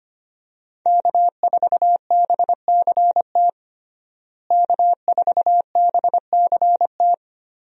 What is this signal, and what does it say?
Morse code, heard as a keyed tone, about 25 wpm, 700 Hz. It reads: K4BCT K4BCT
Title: USA Call Signs - 5 Characters - Repeated 3x - 25wpm